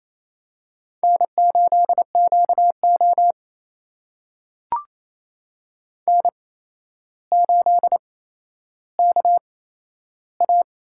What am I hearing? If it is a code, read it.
Morse code, heard as a keyed tone, about 28 wpm, spaced as for 12 wpm, 700 Hz. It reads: N8QO N8KA
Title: USA Call Signs - 4 Characters - 8x Character Spacing - 28wpm